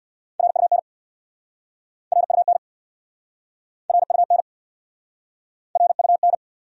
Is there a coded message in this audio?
Yes: Morse code from 0.4 to 6.3 s, heard as a keyed tone, about 69 wpm, 700 Hz. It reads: RUN RUN RUN RUN